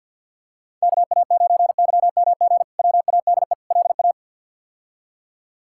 Morse code, heard as a keyed tone, about 53 wpm, 700 Hz. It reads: KA9YMG WADE LA
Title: K1USN SST QSO Practice - 50wpm